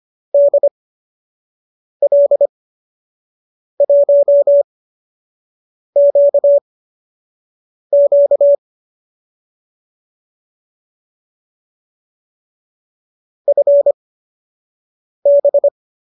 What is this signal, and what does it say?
Morse code, heard as a keyed tone, about 25 wpm, spaced as for 6 wpm, 570 Hz. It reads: DL1QQ FB
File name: qso012 06